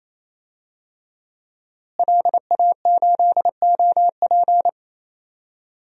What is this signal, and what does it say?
Morse code, heard as a keyed tone, about 28 wpm, 700 Hz. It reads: LA8OP
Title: DXCC-100 Call Signs - 5 Characters - Repeated 3x - 28wpm